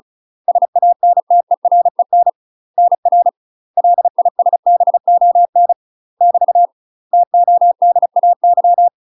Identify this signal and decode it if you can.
Morse code, heard as a keyed tone, about 36 wpm, 700 Hz. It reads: SANTEREN NR LISBON <BT> TODAY
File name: UK MASTER QSOs TRAINING / 35WPM / UK MASTER QSOs 0028